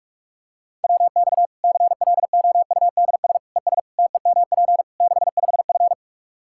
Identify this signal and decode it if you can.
Morse code, heard as a keyed tone, about 46 wpm, 700 Hz. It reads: WX CLOUDS ES TEMP 65F
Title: Practice QSO - Exchange 2 - Info - 45wpm